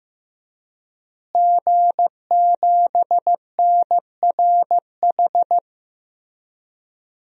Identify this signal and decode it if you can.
Morse code, heard as a keyed tone, about 15 wpm, 700 Hz. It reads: G7NRH
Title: DXCC-100 Call Signs - 5 Characters - Repeated 3x - 15wpm